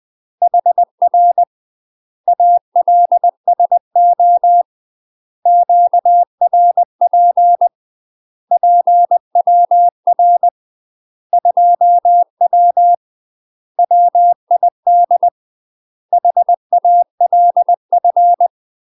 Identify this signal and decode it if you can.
Morse code, heard as a keyed tone, about 20 wpm, 700 Hz. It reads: HR ALSO QRP PWR 2W WID HALF
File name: UK MASTER QSOs TRAINING / 20WPM / UK MASTER QSOs 0015